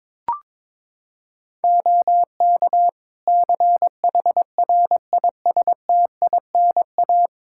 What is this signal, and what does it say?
Morse code, heard as a keyed tone, about 22 wpm, 700 Hz. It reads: OK CHRISTINA
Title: Practice QSO - Exchange 4 - Ending - 22wpm